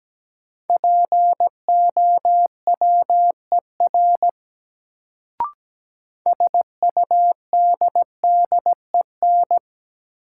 Morse code, heard as a keyed tone, about 17 wpm, 700 Hz. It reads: POWER SUDDEN
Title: Sets of 4 Words - New Word Drill - 17wpm